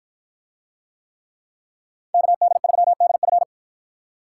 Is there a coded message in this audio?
Yes: Morse code from 2.1 to 3.4 s, heard as a keyed tone, about 48 wpm, 700 Hz. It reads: KD3DF